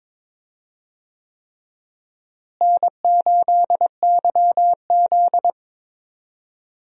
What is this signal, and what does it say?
Morse code, heard as a keyed tone, about 21 wpm, 700 Hz. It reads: N8YZ
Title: USA Call Signs - 4 Characters - Repeated 3x - 22wpm